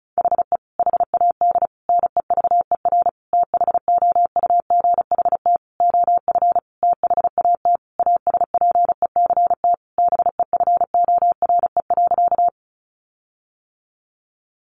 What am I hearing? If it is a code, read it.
Morse code, heard as a keyed tone, about 34 wpm, 700 Hz. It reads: HE HAD NEVER THOUGHT OF THAT ASPECT BEFORE.
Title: Sentences from Top 1300 Words 35wpm